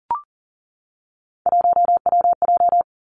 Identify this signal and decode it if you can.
Morse code, heard as a keyed tone, about 39 wpm, 700 Hz. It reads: 1WJ
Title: Three Challenging Letter-Number Combinations 40wpm